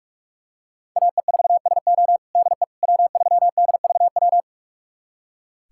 Morse code, heard as a keyed tone, about 46 wpm, 700 Hz. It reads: AE4SO DE W3DVW
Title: USA Callsign DE Callsign - 5 Characters - 45wpm